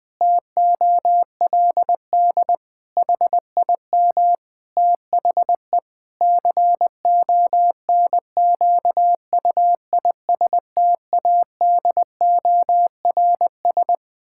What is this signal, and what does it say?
Morse code, heard as a keyed tone, about 20 wpm, 700 Hz. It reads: TOLD HIM THE CONQUISTADORS
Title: Pangram Challenge v2 - 20wpm